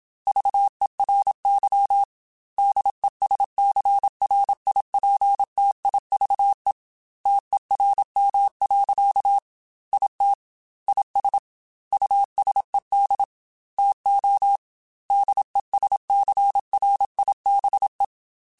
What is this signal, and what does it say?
Morse code, heard as a keyed tone, about 27 wpm, 800 Hz. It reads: VERY DESCRIPTIVE TERM. IT IS USED TO DESCRIBE
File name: Technical Subjects / Technical Subjects-mp3s / Reactance @25